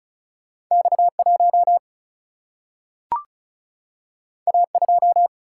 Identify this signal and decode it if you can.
Morse code, heard as a keyed tone, about 36 wpm, 700 Hz. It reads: X 1 A 2
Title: Single-Letter Single-Number 35wpm